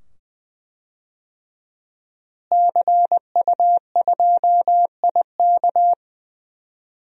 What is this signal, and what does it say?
Morse code, heard as a keyed tone, about 20 wpm, 700 Hz. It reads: CU2IK
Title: DXCC-100 Call Signs - 5 Characters - Repeated 2x - 20wpm